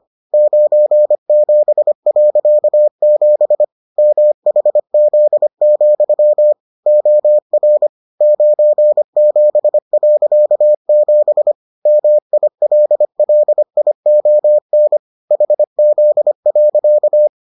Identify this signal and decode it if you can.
Morse code, heard as a keyed tone, about 25 wpm, 600 Hz. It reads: 97.7 MHZ, OR 97.7 MILLION HZ.